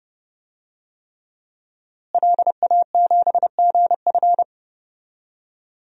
Morse code, heard as a keyed tone, about 32 wpm, 700 Hz. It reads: LA7GF
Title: DXCC-100 Call Signs - 5 Characters - Repeated 3x - 30wpm